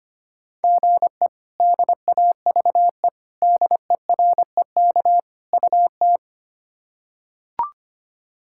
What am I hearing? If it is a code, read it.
Morse code, heard as a keyed tone, about 25 wpm, 700 Hz. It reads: GE DAVE DEREK UT
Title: K1USN SST QSO Practice - 25wpm